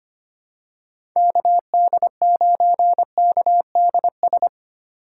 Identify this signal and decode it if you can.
Morse code, heard as a keyed tone, about 25 wpm, 700 Hz. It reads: KD9KDS
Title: USA Call Signs - 6 Characters - 3x Character Spacing - 25wpm